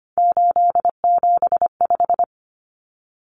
Morse code, heard as a keyed tone, about 25 wpm, 700 Hz. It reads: 875
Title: Numbers - 3-Digit 25wpm